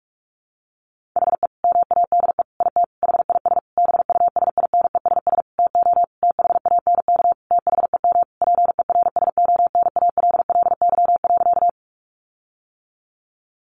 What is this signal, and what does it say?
Morse code, heard as a keyed tone, about 44 wpm, 700 Hz. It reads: HE MADE IT HIS BUSINESS TO THANK THEM PERSONALLY.
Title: Sentences from Top 1400 Words 45wpm